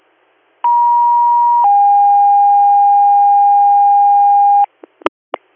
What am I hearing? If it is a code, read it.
Two-tone page, heard as a one-second tone, then a three-second tone, about 950 then 810 Hz. It is heard from 0.6 to 4.7 s.